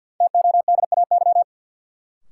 Morse code, heard as a keyed tone, about 52 wpm, 700 Hz. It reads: TODAY
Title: Sentences from Top 600 Words - New Word Drill - 50wpm